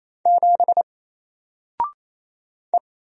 Morse code, heard as a keyed tone, about 27 wpm, 700 Hz. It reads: 7 E
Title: Single Character (Letter:Number:Punctuation:Prosign) 28wpm